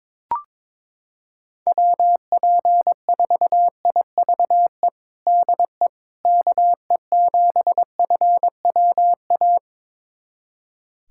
Morse code, heard as a keyed tone, about 22 wpm, 700 Hz. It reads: WP4IVE DE KE7FWA
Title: USA Callsign DE Callsign 22wpm